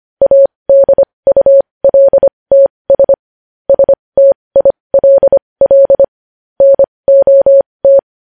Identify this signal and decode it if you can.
Morse code, heard as a keyed tone, about 25 wpm, spaced as for 20 wpm, 550 Hz. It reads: ADULTS STILL NOT
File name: podcast / News headlines-1x-25x20